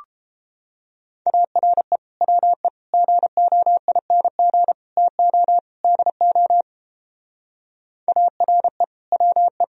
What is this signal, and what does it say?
Morse code, heard as a keyed tone, about 29 wpm, 700 Hz. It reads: ARE WE GOING TO DO ARE WE